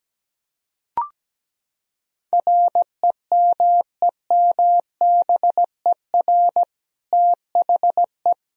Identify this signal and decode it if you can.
Morse code, heard as a keyed tone, about 17 wpm, 700 Hz. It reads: REMEMBER THE
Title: Sentences from Top 1000 Words - Review Part 1 - 17wpm